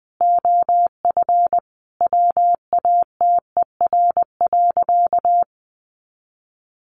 Morse code, heard as a keyed tone, about 20 wpm, 700 Hz. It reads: OF WATER.
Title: Sentences from Top 700 Words 20wpm